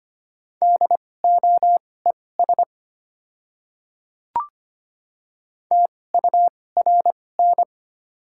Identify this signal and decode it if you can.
Morse code, heard as a keyed tone, about 25 wpm, spaced as for 18 wpm, 690 Hz. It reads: DOES TURN
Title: 2x Character Spacing - Top 300 Words - 25wpm